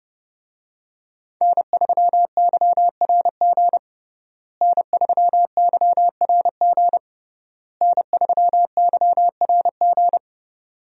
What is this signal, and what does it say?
Morse code, heard as a keyed tone, about 30 wpm, 700 Hz. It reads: N3YRG N3YRG N3YRG
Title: USA Call Signs - 5 Characters - Repeated 3x - 30wpm